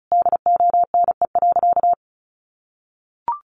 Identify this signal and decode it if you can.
Morse code, heard as a keyed tone, about 34 wpm, 700 Hz. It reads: DONE.
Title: Sentences from Top 500 Words 35wpm